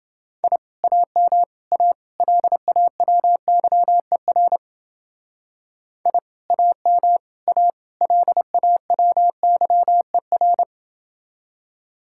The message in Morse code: I AM A LAWYER I AM A LAWYER

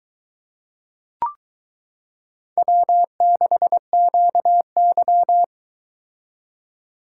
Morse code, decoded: W6QY

